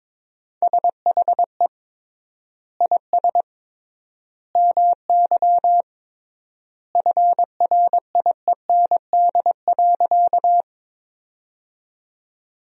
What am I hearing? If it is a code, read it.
Morse code, heard as a keyed tone, about 22 wpm, 700 Hz. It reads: SHE IS MY FRIEND.
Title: 3x Word-Spacing - Sentences from Top 300 Words - 22wpm